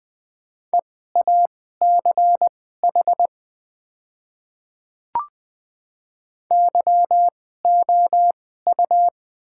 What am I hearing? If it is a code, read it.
Morse code, heard as a keyed tone, about 20 wpm, spaced as for 15 wpm, 690 Hz. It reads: EACH YOU
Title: 2x Character Spacing - Top 100 Words - 20wpm